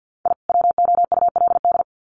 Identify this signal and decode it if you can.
Morse code, heard as a keyed tone, about 48 wpm, 700 Hz. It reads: I WOULD
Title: Sets of 2 Words 50wpm